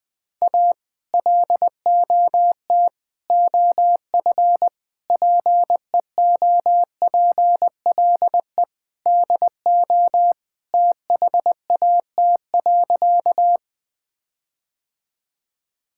Morse code, heard as a keyed tone, about 20 wpm, 700 Hz. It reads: A LOT OF PEOPLE DO THAT.
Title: Sentences with 5-Word Phrases - 20wpm